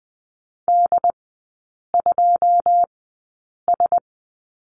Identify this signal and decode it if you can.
Morse code, heard as a keyed tone, about 20 wpm, spaced as for 8 wpm, 700 Hz. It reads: D2S
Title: 100 Lesson 17e - Letter u - Three Characters 20wpm